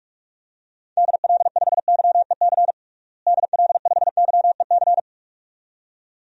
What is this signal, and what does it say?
Morse code, heard as a keyed tone, about 46 wpm, 700 Hz. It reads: DL5YEC DL5YEC
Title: DXCC-100 Call Signs - 6 Characters - Repeated 2x - 45wpm